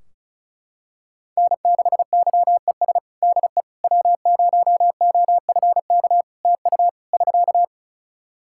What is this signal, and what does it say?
Morse code, heard as a keyed tone, about 36 wpm, 700 Hz. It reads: N6YES DE W0OFK TU <SK>
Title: K1USN SST QSO Practice 35wpm